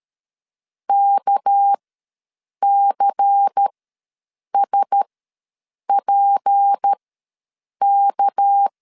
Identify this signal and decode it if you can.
Morse code, heard as a keyed tone, about 13 wpm, spaced as for 7 wpm, 800 Hz. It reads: KCSPK
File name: LEZIONI / 6 PKX0 / 6 PK ws 12wpm 3116